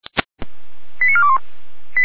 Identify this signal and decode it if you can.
Five-tone selective call (ZVEI), heard as quick runs of five tones, tones of about 70 ms. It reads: 89421
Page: /455/SDR/ZVEI/88409/